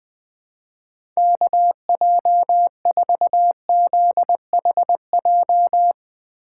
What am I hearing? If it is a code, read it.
Morse code, heard as a keyed tone, about 20 wpm, 700 Hz. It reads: KJ4ZHJ
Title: USA Call Signs - 6 Characters - 2x Character Spacing - 20wpm